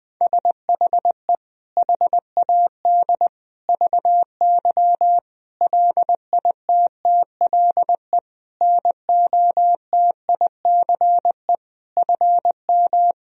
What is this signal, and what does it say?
Morse code, heard as a keyed tone, about 20 wpm, 700 Hz. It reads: SHE HAD VY LITTLE NOTICE FM
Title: Sentences from Top 400 Words and Abbreviations 20wpm